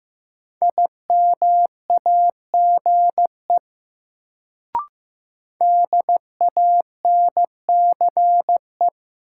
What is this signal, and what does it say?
Morse code, heard as a keyed tone, about 15 wpm, 700 Hz. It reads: IMAGE DANCE
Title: Sentences from Top 700 Words - New Word Drill - 15wpm